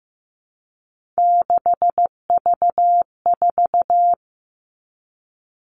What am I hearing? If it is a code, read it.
Morse code, heard as a keyed tone, about 15 wpm, 700 Hz. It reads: 6V4
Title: Three Challenging Letter-Number Combinations 15wpm